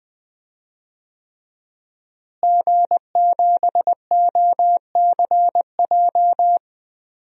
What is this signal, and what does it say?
Morse code, heard as a keyed tone, about 20 wpm, 700 Hz. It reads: G7OCJ
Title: DXCC-100 Call Signs - 5 Characters - Repeated 3x - 20wpm